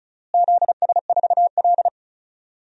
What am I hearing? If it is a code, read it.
Morse code, heard as a keyed tone, about 34 wpm, 700 Hz. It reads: ZS4L
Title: DXCC-100 Call Signs - 4 Characters - Repeated 2x - 35wpm